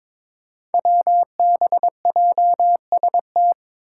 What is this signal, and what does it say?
Morse code, heard as a keyed tone, about 22 wpm, 700 Hz. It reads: WBJST
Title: Five Challenging Letter-Number Combinations 22wpm